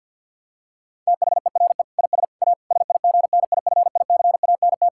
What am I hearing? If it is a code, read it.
Morse code, heard as a keyed tone, about 46 wpm, 700 Hz. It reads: THERE IS A SIGNIFICANT
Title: Sentences from Top 1000 Words - Review Part 6 - 50wpm